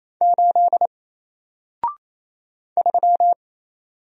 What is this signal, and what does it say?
Morse code, heard as a keyed tone, about 28 wpm, 700 Hz. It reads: 8 3
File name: Single-Letter-Number-28wpm